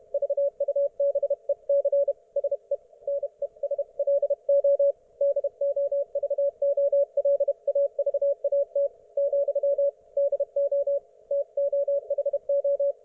Morse code, heard as a keyed tone, about 32 wpm, 560 Hz. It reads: VUBEC SE NESLO DOVOLAVAT , DO TOHO